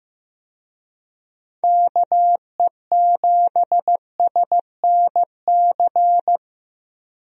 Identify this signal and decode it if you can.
Morse code, heard as a keyed tone, about 15 wpm, 700 Hz. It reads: KE7SNC
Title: USA Call Signs - 6 Characters - 15wpm